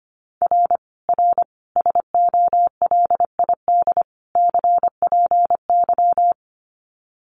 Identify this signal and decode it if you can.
Morse code, heard as a keyed tone, about 25 wpm, 700 Hz. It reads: R R SOLID CPY
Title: Common QSO Phrases 25wpm